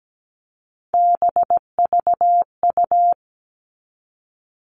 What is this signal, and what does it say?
Morse code, heard as a keyed tone, about 17 wpm, 700 Hz. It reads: BVU
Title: All Cut Numbers - 3-Digit 17wpm